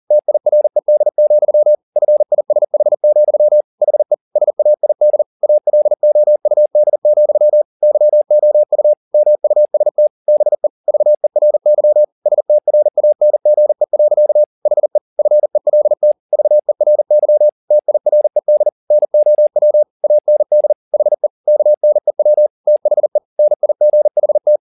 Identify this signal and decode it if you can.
Morse code, heard as a keyed tone, about 40 wpm, 600 Hz. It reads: TIRED, FISH, HE SAID ALOUD, YOU MUST BE VERY STRANGE. HE FELT VERY TIRED NOW AND HE KNEW THE NIGHT